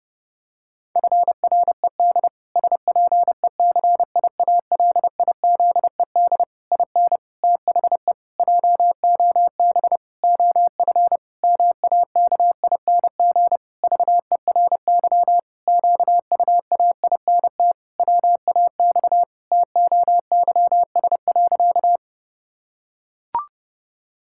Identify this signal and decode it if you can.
Morse code, heard as a keyed tone, about 32 wpm, 700 Hz. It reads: FRED SPECIALIZED IN THE JOB OF MAKING VERY QUAINT WAX TOYS.
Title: Pangram Challenge v2 - 30wpm